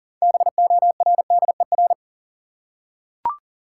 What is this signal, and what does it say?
Morse code, heard as a keyed tone, about 40 wpm, 700 Hz. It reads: BORDER
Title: Six Letter Words - 40wpm